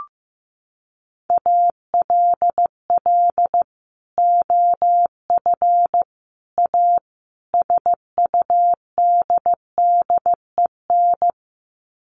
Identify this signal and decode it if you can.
Morse code, heard as a keyed tone, about 15 wpm, 700 Hz. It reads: ALL OF A SUDDEN
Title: Sets of 4 Words 15wpm